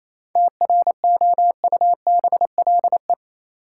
Morse code, heard as a keyed tone, about 28 wpm, 700 Hz. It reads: TROUBLE
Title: Sentences from Top 1200 Words - New Word Drill Alt - 28wpm